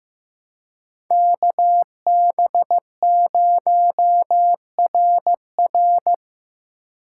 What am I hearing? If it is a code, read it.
Morse code, heard as a keyed tone, about 15 wpm, 700 Hz. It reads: KB0RR
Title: USA Call Signs - 5 Characters - 3x Character Spacing - 15wpm